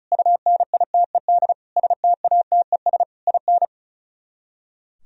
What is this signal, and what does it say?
Morse code, heard as a keyed tone, about 46 wpm, 700 Hz. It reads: UNITED STATES IN